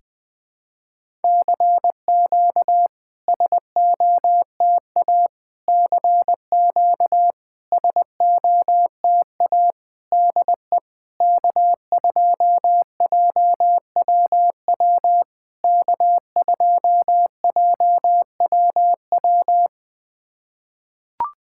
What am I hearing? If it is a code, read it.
Morse code, heard as a keyed tone, about 21 wpm, 700 Hz. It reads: CQ SOTA CQ SOTA DE K2JWW K2JWW
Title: SOTA QSO Practice 20wpm